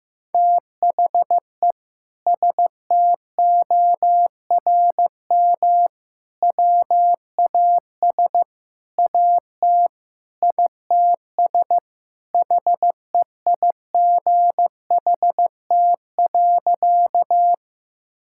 Morse code, heard as a keyed tone, about 16 wpm, 700 Hz. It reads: THE STORM WAS AT ITS HEIGHT.
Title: Sentences from Top 1600 Words 15wpm